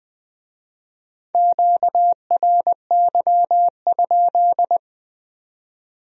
Morse code, heard as a keyed tone, about 19 wpm, 700 Hz. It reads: QRY?